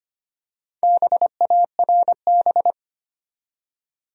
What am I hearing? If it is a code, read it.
Morse code, heard as a keyed tone, about 25 wpm, 700 Hz. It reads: BARB
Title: USA Common Names - 25wpm